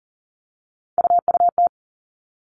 Morse code, heard as a keyed tone, about 40 wpm, 700 Hz. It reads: UUT
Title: All Cut Numbers - 3-Digit 40wpm